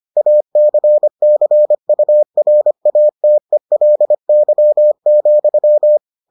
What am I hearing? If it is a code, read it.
Morse code, heard as a keyed tone, about 25 wpm, 600 Hz. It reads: ACCURATELY,